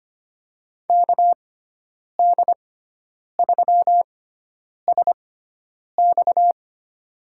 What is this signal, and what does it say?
Morse code, heard as a keyed tone, about 25 wpm, spaced as for 9 wpm, 700 Hz. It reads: KD3SX
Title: USA Call Signs - 5 Characters - 6x Character Spacing - 25wpm